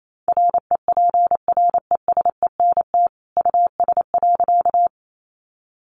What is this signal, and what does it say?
Morse code, heard as a keyed tone, about 28 wpm, 700 Hz. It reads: REPRESENT US.
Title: Sentences from Top 1200 Words 28wpm